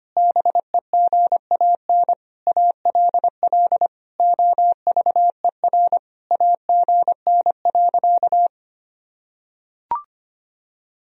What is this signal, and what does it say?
Morse code, heard as a keyed tone, about 25 wpm, 700 Hz. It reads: BEGAN ALL OVER AGN.
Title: Sentences from Top 500 Words and Abbreviations 25wpm